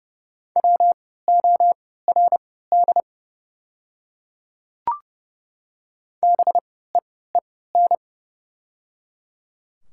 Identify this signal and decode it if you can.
Morse code, heard as a keyed tone, about 30 wpm, spaced as for 17 wpm, 690 Hz. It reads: WORD BEEN
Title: 3x Character Spacing - Top 200 Words - 30wpm